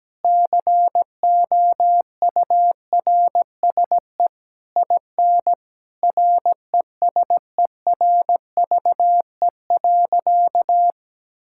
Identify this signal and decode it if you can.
Morse code, heard as a keyed tone, about 17 wpm, 700 Hz. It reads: COURSE IN RESERVE.
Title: Sentences from Top 1000 Words - Review Part 6 - 17wpm